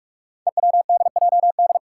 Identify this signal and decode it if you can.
Morse code, heard as a keyed tone, about 44 wpm, 700 Hz. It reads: EWDJD
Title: Five Challenging Letter-Number Combinations 45wpm